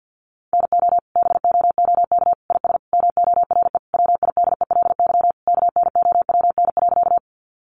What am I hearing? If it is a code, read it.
Morse code, heard as a keyed tone, about 50 wpm, 700 Hz. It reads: NO BOOK IS MORE WIDELY KNOWN.
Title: Sentences from Top 1200 Words 50wpm